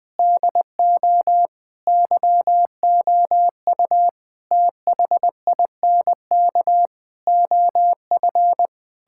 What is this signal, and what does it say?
Morse code, heard as a keyed tone, about 21 wpm, 700 Hz. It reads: DO YOU THINK OF
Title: Sentences from Top 1600 Words 20wpm